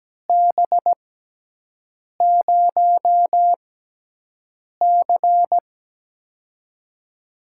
Morse code, heard as a keyed tone, about 17 wpm, spaced as for 6 wpm, 700 Hz. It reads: B0C
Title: USA Call Signs - 6x Character Spacing - 17wpm